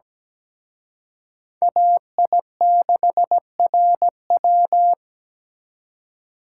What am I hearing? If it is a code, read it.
Morse code, heard as a keyed tone, about 20 wpm, 700 Hz. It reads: AI6RW
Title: Speed Racing - USA Call Signs 17wpm